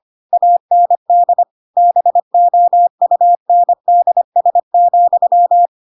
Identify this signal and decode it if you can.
Morse code, heard as a keyed tone, about 25 wpm, 700 Hz. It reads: AND BOUNDS,
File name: Pinocchio / Pinocchio ch3 25 25 ik1hge